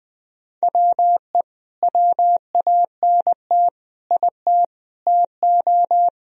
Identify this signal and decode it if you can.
Morse code, heard as a keyed tone, about 20 wpm, 700 Hz. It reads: WE WANT IT TO
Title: Sentences from Top 1000 Words - Review Part 1 - 20wpm